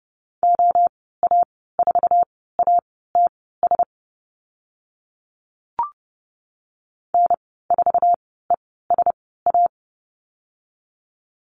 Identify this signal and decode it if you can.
Morse code, heard as a keyed tone, about 30 wpm, spaced as for 17 wpm, 700 Hz. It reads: OA4ATS N4ESA
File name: Lesson 14g - Number 2 - Call Signs 30wpm